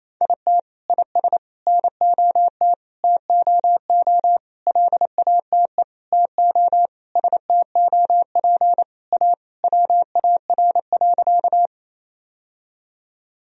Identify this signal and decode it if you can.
Morse code, heard as a keyed tone, about 28 wpm, 700 Hz. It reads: IT IS NOT TOO LATE TO STOP A WAR.
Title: Sentences from Top 1000 Words - Review Part 2 - 28wpm